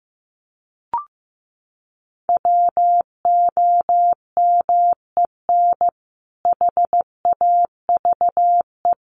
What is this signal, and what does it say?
Morse code, heard as a keyed tone, about 15 wpm, 700 Hz. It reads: WOMEN HAVE
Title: Sentences from Top 1400 Words 15wpm